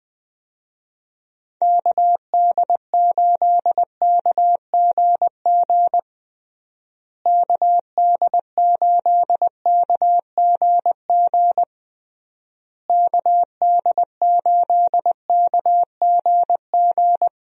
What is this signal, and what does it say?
Morse code, heard as a keyed tone, about 20 wpm, 700 Hz. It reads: KD8KGG KD8KGG KD8KGG
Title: USA Call Signs - 6 Characters - Repeated 3x - 20wpm